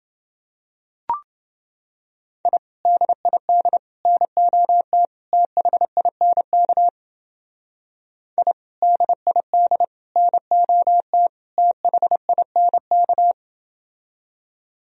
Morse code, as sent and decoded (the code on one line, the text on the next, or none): .. / -.. .. -.. / -. --- - / - .... .. -. -.- / .. / -.. .. -.. / -. --- - / - .... .. -. -.-
I DID NOT THINK I DID NOT THINK